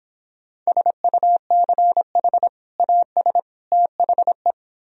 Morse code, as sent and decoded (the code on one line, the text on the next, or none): ... ..- -.-. .... / .- ... / - .... .
SUCH AS THE